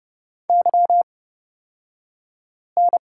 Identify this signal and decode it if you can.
Morse code, heard as a keyed tone, about 30 wpm, spaced as for 5 wpm, 700 Hz. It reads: YN
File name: Lesson 38d - Letter q - Two Characters 30wpm